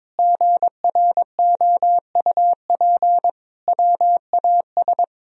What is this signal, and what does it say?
Morse code, heard as a keyed tone, about 22 wpm, 700 Hz. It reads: GROUP WAS
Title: Sentences from Top 1000 Words - Review Part 1 - 22wpm